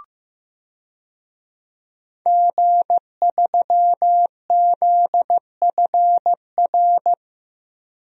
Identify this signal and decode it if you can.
Morse code, heard as a keyed tone, about 15 wpm, 700 Hz. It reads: G3ZFR
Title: DXCC-100 Call Signs - 5 Characters - 15wpm